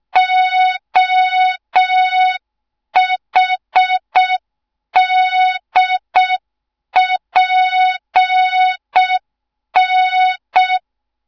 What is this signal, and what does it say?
Morse code, heard as a keyed tone, about 6 wpm, 3700 Hz. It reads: OHDPN
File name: Morze-HA2MN / hang / 07 GYAKORLAT